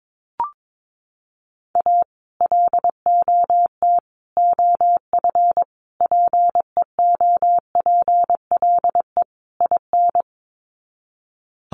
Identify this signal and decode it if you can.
Morse code, heard as a keyed tone, about 22 wpm, 700 Hz. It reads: A LOT OF PEOPLE IN
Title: Sets of 5 Words 22wpm